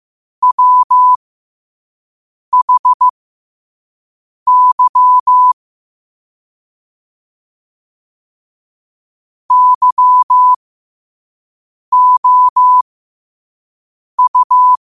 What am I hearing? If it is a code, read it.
Morse code, heard as a keyed tone, about 15 wpm, spaced as for 5 wpm, 1000 Hz. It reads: WHY YOU
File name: qotdfiles / 5wpm